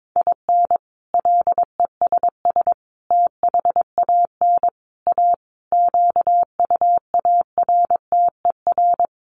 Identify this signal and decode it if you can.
Morse code, heard as a keyed tone, about 22 wpm, 700 Hz. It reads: IN LESS THAN A QUARTER
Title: Sentences from Top 1300 Words 22wpm